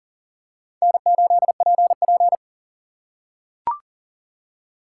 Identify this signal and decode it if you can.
Morse code, heard as a keyed tone, about 40 wpm, 700 Hz. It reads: N8PP
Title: USA Call Signs - 4 Characters - 3x Character Spacing - 40wpm